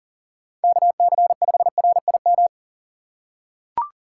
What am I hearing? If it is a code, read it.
Morse code, heard as a keyed tone, about 42 wpm, 700 Hz. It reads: KC5RIM
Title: USA Call Signs - 4x Character Spacing - 40wpm